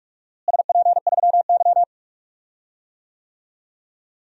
Morse code, heard as a keyed tone, about 46 wpm, 700 Hz. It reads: SP3Y
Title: DXCC-100 Call Signs - 4 Characters - Repeated 3x - 45wpm